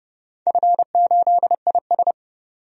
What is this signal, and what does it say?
Morse code, heard as a keyed tone, about 32 wpm, 700 Hz. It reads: F8IS
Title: DXCC-100 Call Signs - 4 Characters - 30wpm